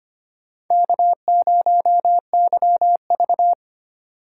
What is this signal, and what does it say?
Morse code, heard as a keyed tone, about 33 wpm, 700 Hz. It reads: K0YV